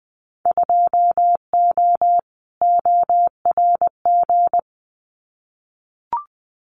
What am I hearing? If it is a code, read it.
Morse code, heard as a keyed tone, about 20 wpm, 700 Hz. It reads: 2O ORG
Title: Winter Field Day Exchanges 20wpm